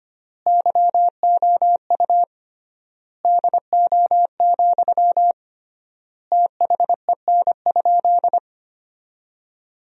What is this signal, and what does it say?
Morse code, heard as a keyed tone, about 25 wpm, 700 Hz. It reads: YOU DO, THEN?
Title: 3x Word-Spacing - Sentences from Top 100 Words - 25wpm